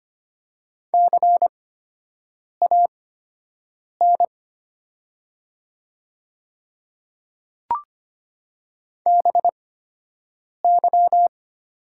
Morse code, heard as a keyed tone, about 25 wpm, spaced as for 7 wpm, 700 Hz. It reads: CAN BY
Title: 8x Character Spacing - Top 100 Words - 25wpm